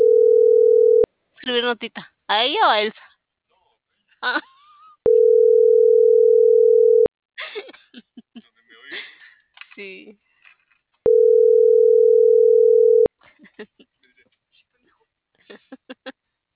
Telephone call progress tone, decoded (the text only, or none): ringback tone